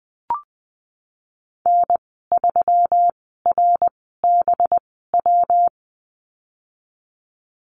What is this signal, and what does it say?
Morse code, heard as a keyed tone, about 19 wpm, spaced as for 14 wpm, 700 Hz. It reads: N3RBW
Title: 200 Lesson 32g - Letter k - Call Signs 20wpm